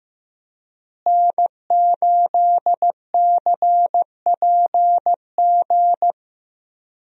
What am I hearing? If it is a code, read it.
Morse code, heard as a keyed tone, about 15 wpm, 700 Hz. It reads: N8CPG
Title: USA Call Signs - 5 Characters - Repeated 3x - 15wpm